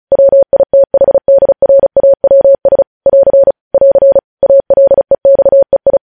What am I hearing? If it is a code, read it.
Morse code, heard as a keyed tone, about 36 wpm, 550 Hz. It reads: WITHDRAWS <AR> <AR> ALEXEI
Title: News Headlines at 35